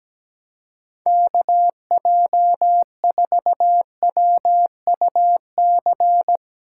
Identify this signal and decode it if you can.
Morse code, heard as a keyed tone, about 17 wpm, 700 Hz. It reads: KJ4WUC
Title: USA Call Signs - 4x Character Spacing - 17wpm